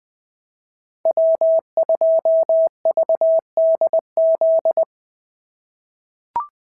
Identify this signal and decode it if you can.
Morse code, heard as a keyed tone, about 20 wpm, 650 Hz. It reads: W2VDZ
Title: POTA QSO Practice - 20wpm